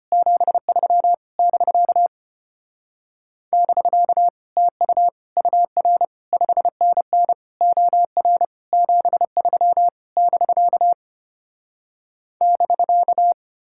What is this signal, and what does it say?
Morse code, heard as a keyed tone, about 33 wpm, 700 Hz. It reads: 73 <BK> <BK> TU UR 5NN OR 73 <BK> <BK>